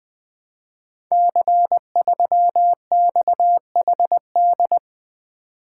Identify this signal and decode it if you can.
Morse code, heard as a keyed tone, about 20 wpm, 700 Hz. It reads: C3XHD
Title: Code Groups - 5 Letter:Numbers - 20wpm